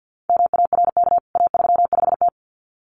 Morse code, heard as a keyed tone, about 50 wpm, 700 Hz. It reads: MARK A35T
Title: CWT Exchange 50wpm